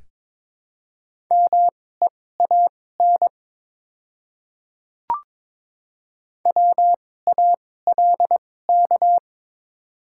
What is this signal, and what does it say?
Morse code, heard as a keyed tone, about 22 wpm, spaced as for 16 wpm, 700 Hz. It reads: MEAN WALK
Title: 2x Character Spacing - Top 300 Words - 22wpm